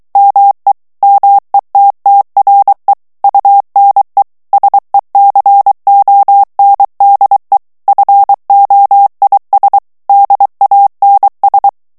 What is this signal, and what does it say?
Morse code, heard as a keyed tone, about 23 wpm, 800 Hz. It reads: ME METTRE UNE SECONDE FOIS DANS
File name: morse mp3 / FRANCAIS TEXTE CLAIR 2 VITESSE 22 MOTS MINUTES